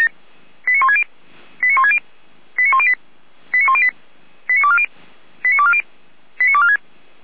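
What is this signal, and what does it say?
Five-tone selective call (ZVEI), heard as quick runs of five tones, tones of about 70 ms. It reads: 89160 89160 89108 89108 89240 89240 89246
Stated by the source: Five-tone ZVEI selective call received by SDR.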